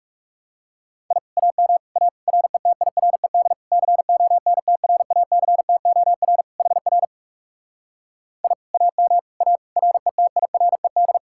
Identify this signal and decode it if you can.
Morse code, heard as a keyed tone, about 40 wpm, 700 Hz. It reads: I AM A RETIRED CONTRACTOR HR I AM A RETIRED